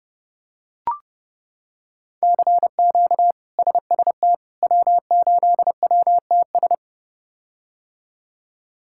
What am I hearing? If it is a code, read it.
Morse code, heard as a keyed tone, about 30 wpm, 700 Hz. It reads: CQ SST W8WTS
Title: K1USN SST QSO Practice - 30wpm